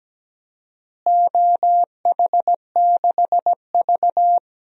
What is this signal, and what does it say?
Morse code, heard as a keyed tone, about 17 wpm, 700 Hz. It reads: OH6V
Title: DXCC-100 Call Signs - 4 Characters - Repeated 2x - 17wpm